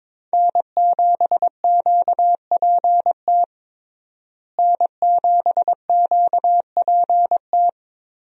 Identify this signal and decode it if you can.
Morse code, heard as a keyed tone, about 22 wpm, 700 Hz. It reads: N7QPT N7QPT
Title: USA Call Signs - 5 Characters - Repeated 3x - 22wpm